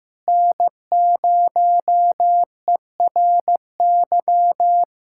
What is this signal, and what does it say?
Morse code, heard as a keyed tone, about 16 wpm, 700 Hz. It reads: N0ERY
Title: USA Callsign DE Callsign - 5 Characters - 15wpm